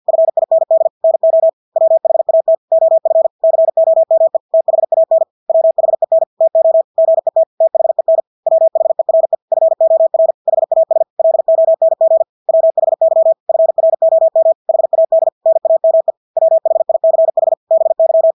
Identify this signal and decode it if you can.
Morse code, heard as a keyed tone, about 50 wpm, 650 Hz. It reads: FIND NO WHAT OF COME THAN WHEN TO GET THEN WHERE FOR HAS LONG WHY FROM HAD NAME WHICH BY
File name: Engelske ord 50 wpm